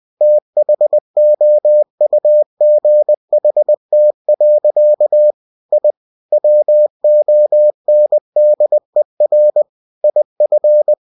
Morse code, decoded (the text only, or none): THOUGHT. I WONDER IF